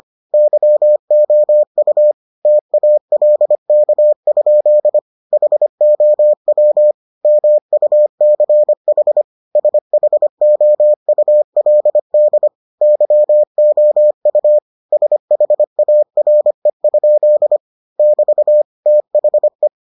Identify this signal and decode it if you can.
Morse code, heard as a keyed tone, about 25 wpm, 600 Hz. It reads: YOU TALK? HOW MUCH SHOULD YOU SHARE? <BT> THE